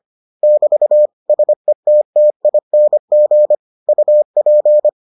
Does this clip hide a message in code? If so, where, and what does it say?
Yes: Morse code from 0.4 to 4.9 s, heard as a keyed tone, about 25 wpm, 600 Hz. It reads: <BT> SETTING UP